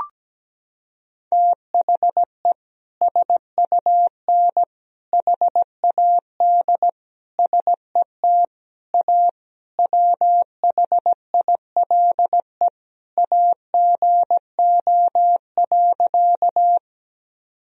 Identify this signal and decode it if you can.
Morse code, heard as a keyed tone, about 17 wpm, 700 Hz. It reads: THE SUN HAD SET A WHILE AGO.
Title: Sentences from Top 1000 Words - Review Part 1 - 17wpm